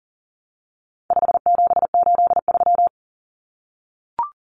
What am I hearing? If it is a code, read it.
Morse code, heard as a keyed tone, about 40 wpm, 700 Hz. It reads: 5783
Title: Numbers - 4-Digit 40wpm